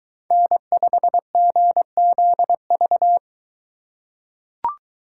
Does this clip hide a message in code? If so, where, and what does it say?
Yes: Morse code from 0.3 to 3.2 s, heard as a keyed tone, about 23 wpm, 700 Hz. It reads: N5GZV